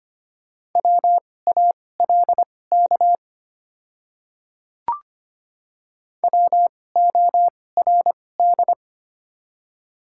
Morse code, heard as a keyed tone, about 25 wpm, spaced as for 18 wpm, 700 Hz. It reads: WALK WORD
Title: 2x Character Spacing - Top 300 Words - 25wpm